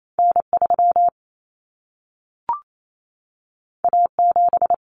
Morse code, heard as a keyed tone, about 28 wpm, 700 Hz. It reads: N3 A7
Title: Common Cut Numbers - 2-Digit 28wpm